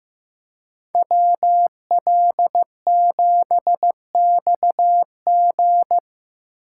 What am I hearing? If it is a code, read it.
Morse code, heard as a keyed tone, about 18 wpm, 700 Hz. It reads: WL7XG
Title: Speed Racing - USA Call Signs 15wpm